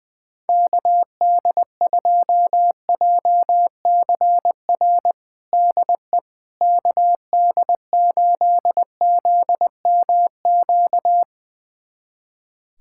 Morse code, heard as a keyed tone, about 20 wpm, 690 Hz. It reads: KD2JCR DE KD8ZMQ
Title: USA Callsign DE Callsign 20wpm